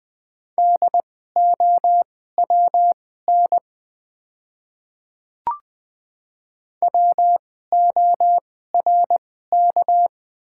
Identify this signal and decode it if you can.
Morse code, heard as a keyed tone, about 20 wpm, spaced as for 14 wpm, 700 Hz. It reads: DOWN WORK
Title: 2x Character Spacing - Top 200 Words - 20wpm